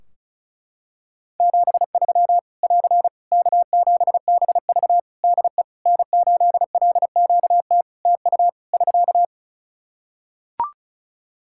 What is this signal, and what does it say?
Morse code, heard as a keyed tone, about 34 wpm, 700 Hz. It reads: 73 <AR> K7BV DE N8LQT TU <SK>
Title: Practice QSO - Exchange 4 - Ending - 35wpm